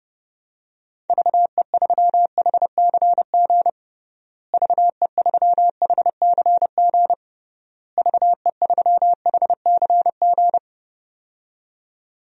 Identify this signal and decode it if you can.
Morse code, heard as a keyed tone, about 32 wpm, 700 Hz. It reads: VE3HCG VE3HCG VE3HCG
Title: DXCC-100 Call Signs - 6 Characters - Repeated 3x - 30wpm